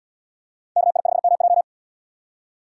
Morse code, heard as a keyed tone, about 68 wpm, 700 Hz. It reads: KI5MJ